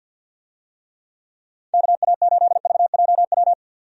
Morse code, heard as a keyed tone, about 47 wpm, 700 Hz. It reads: KA8VJW
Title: USA Call Signs - 6 Characters - 50wpm